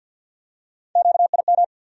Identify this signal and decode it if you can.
Morse code, heard as a keyed tone, about 52 wpm, 700 Hz. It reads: QIM
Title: USA Call Sign Postfixes - 3 Letters 50wpm